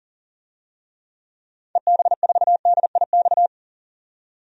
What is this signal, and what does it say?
Morse code, heard as a keyed tone, about 40 wpm, 700 Hz. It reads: EB4DIX
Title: DXCC-100 Call Signs - 6 Characters - 40wpm